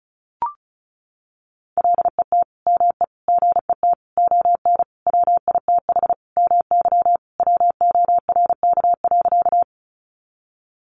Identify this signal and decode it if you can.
Morse code, heard as a keyed tone, about 34 wpm, 700 Hz. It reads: LET ME GET ON WITH MY WORK.
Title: Sentences from Top 300 Words 35wpm